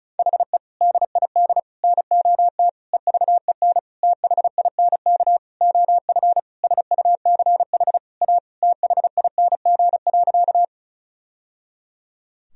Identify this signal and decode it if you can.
Morse code, heard as a keyed tone, about 34 wpm, 700 Hz. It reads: HE DID NOT EVEN THINK OF SUCH A THING.
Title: Sentences from Top 1000 Words - Review Part 1 - 35wpm